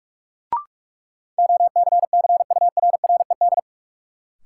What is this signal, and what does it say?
Morse code, heard as a keyed tone, about 46 wpm, 700 Hz. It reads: OCCURRED
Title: Sets of 5 Words Encore - New Word Drill - 45wpm